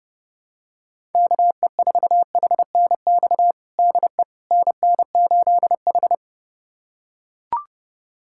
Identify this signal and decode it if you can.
Morse code, heard as a keyed tone, about 30 wpm, 700 Hz. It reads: KE4HNX DE NN8H
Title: USA Callsign DE Callsign 30wpm